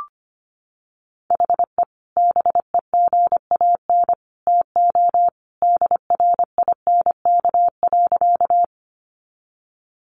Morse code, heard as a keyed tone, about 25 wpm, 700 Hz. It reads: HE BEGAN TO DRINK.
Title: Sentences from Top 1500 Words 25wpm